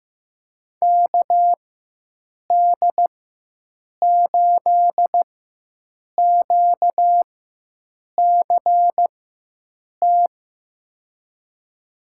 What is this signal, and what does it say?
Morse code, heard as a keyed tone, about 15 wpm, spaced as for 7 wpm, 700 Hz. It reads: KD8QCT
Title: USA Call Signs - 4x Character Spacing - 15wpm